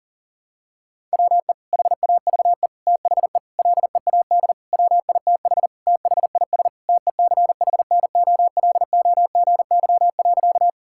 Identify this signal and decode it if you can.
Morse code, heard as a keyed tone, about 37 wpm, 700 Hz. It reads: WE HAVE THE LEAD WITH THIS TECHNOLOGY.
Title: Sentences from Top 1000 Words - Review Part 5 - 40wpm